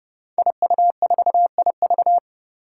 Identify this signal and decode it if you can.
Morse code, heard as a keyed tone, about 30 wpm, 700 Hz. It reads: IU4IV
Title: Five Challenging Letter-Number Combinations 30wpm